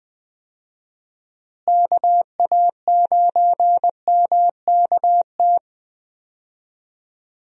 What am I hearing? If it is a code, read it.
Morse code, heard as a keyed tone, about 21 wpm, 700 Hz. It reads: KA9MKT
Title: USA Call Signs - 6 Characters - Repeated 2x - 20wpm